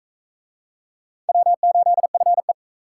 Morse code, heard as a keyed tone, about 44 wpm, 700 Hz. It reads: W8FE